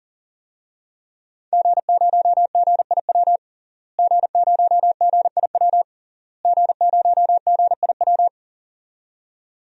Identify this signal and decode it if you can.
Morse code, heard as a keyed tone, about 37 wpm, 700 Hz. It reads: G0GIW G0GIW G0GIW
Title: DXCC-100 Call Signs - 5 Characters - Repeated 3x - 40wpm